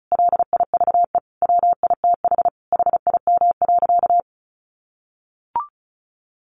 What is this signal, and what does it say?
Morse code, heard as a keyed tone, about 34 wpm, 700 Hz. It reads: LIVE WITH HIM.
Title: Sentences from Top 1100 Words 35wpm